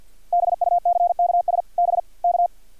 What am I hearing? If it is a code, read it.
Morse code, heard as a keyed tone, about 48 wpm, 630 Hz. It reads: BUCKS BK
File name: archived / CRSnet